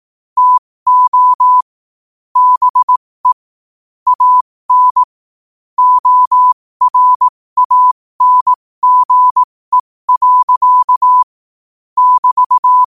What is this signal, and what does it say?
Morse code, heard as a keyed tone, about 18 wpm, spaced as for 15 wpm, 1000 Hz. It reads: TO BE AN ORANGE. <BT>
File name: qotdfiles / 15wpm